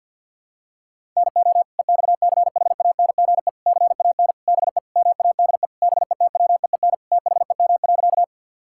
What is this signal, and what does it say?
Morse code, heard as a keyed tone, about 50 wpm, 700 Hz. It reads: NO EXCHANGE CAN BE MADE BETWEEN THEM.
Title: Sentences from Top 1000 Words - Review Part 6 - 50wpm